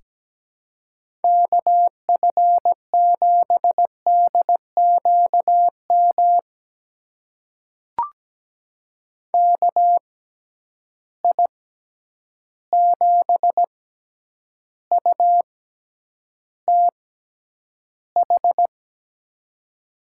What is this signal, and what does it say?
Morse code, heard as a keyed tone, about 17 wpm, spaced as for 9 wpm, 700 Hz. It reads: KF7DQM KI7UTH
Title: USA Call Signs - 6 Characters - 6x Character Spacing - 17wpm